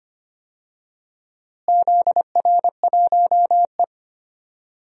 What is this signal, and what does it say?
Morse code, heard as a keyed tone, about 25 wpm, 700 Hz. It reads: ZR1E
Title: DXCC-100 Call Signs - 4 Characters - Repeated 2x - 25wpm